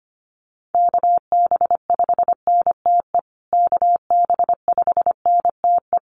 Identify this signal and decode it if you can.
Morse code, heard as a keyed tone, about 25 wpm, 700 Hz. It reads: KB5NTE KB5NTE
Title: USA Call Signs Repeated 3x 25wpm